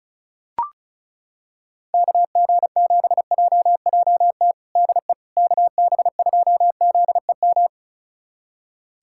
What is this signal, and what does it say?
Morse code, heard as a keyed tone, about 34 wpm, 700 Hz. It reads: KG7JJT DE KB2ZEM
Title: USA Callsign DE Callsign - 6 Characters - 35wpm